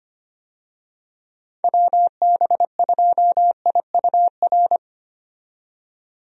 Morse code, heard as a keyed tone, about 25 wpm, 700 Hz. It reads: WB2IUR
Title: USA Call Signs - 6 Characters - 25wpm